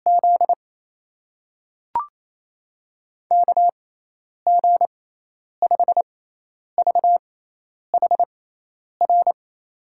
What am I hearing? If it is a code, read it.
Morse code, heard as a keyed tone, about 28 wpm, spaced as for 10 wpm, 700 Hz. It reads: Z KG5VHR
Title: USA Call Signs - 6x Character Spacing - 28wpm